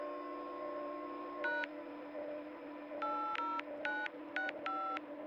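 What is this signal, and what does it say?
DTMF, heard as telephone touch tones, tones of about 220 ms. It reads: # 80965
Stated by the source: DTMF from a telephone dial-up audiobook.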